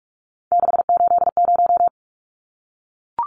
Morse code, heard as a keyed tone, about 44 wpm, 700 Hz. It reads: 680
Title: Numbers - 3-Digit 45wpm